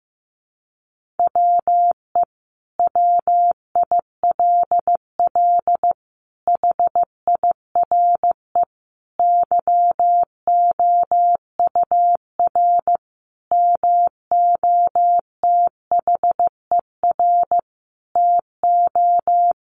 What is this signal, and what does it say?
Morse code, heard as a keyed tone, about 14 wpm, 700 Hz. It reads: WE WILL HIRE YOUR MOTHER TO
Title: Sentences from Top 1400 Words 15wpm